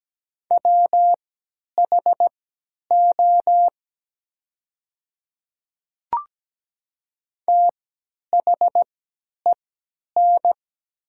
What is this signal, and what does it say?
Morse code, heard as a keyed tone, about 17 wpm, spaced as for 10 wpm, 700 Hz. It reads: WHO THEN
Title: 3x Character Spacing - Top 100 Words - 17wpm